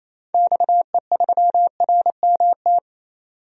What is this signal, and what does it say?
Morse code, heard as a keyed tone, about 28 wpm, 700 Hz. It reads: XE3RMT
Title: DXCC-100 Call Signs - 6 Characters - 28wpm